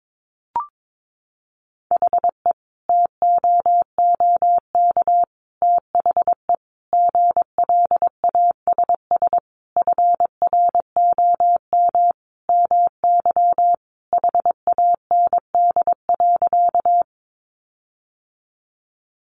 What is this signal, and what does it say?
Morse code, heard as a keyed tone, about 22 wpm, 700 Hz. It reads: HE TOOK THE GLASS FROM MY HAND.
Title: Sentences from Top 1300 Words 22wpm